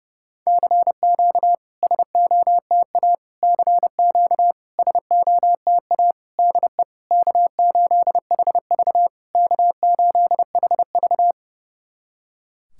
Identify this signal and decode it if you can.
Morse code, heard as a keyed tone, about 30 wpm, 700 Hz. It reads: CQ SOTA CQ SOTA DE K8HV K8HV
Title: SOTA QSO Practice 30wpm